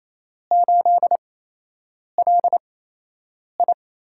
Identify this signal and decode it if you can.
Morse code, heard as a keyed tone, about 27 wpm, spaced as for 8 wpm, 700 Hz. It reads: 8LI
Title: USA Call Signs - 4 Characters - 8x Character Spacing - 28wpm